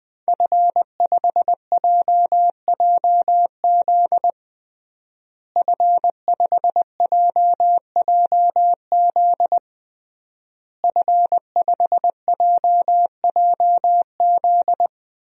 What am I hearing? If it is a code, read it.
Morse code, heard as a keyed tone, about 20 wpm, 700 Hz. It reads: F5JJZ F5JJZ F5JJZ
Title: DXCC-100 Call Signs - 5 Characters - Repeated 3x - 20wpm